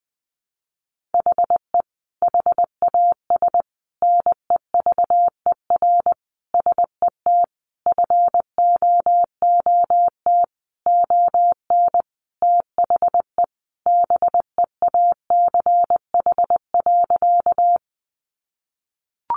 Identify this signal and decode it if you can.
Morse code, heard as a keyed tone, about 20 wpm, 700 Hz. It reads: HE HAS NEVER SET FOOT ON THE BEACH.
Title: Sentences from Top 1500 Words 20wpm